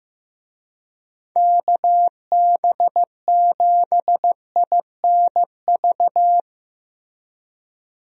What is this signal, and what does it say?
Morse code, heard as a keyed tone, about 16 wpm, 700 Hz. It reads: KB7INV
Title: USA Call Signs - 6 Characters - Repeated 2x - 15wpm